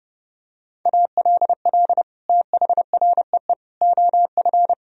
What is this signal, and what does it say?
Morse code, heard as a keyed tone, about 30 wpm, 700 Hz. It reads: ALL THREE OF
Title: Sentences from Top 1000 Words - Review Part 1 - 30wpm